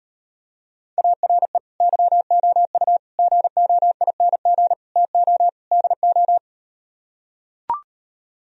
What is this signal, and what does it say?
Morse code, heard as a keyed tone, about 38 wpm, 700 Hz. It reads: ARE YOU GOING TO DO